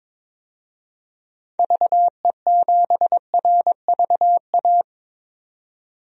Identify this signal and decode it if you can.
Morse code, heard as a keyed tone, about 22 wpm, 700 Hz. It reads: VE7RVA
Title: DXCC-100 Call Signs - 6 Characters - 22wpm